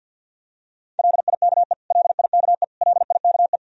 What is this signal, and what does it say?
Morse code, heard as a keyed tone, about 51 wpm, 700 Hz. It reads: LIKE LIKE LIKE
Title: Top 100 Words Repeated 50wpm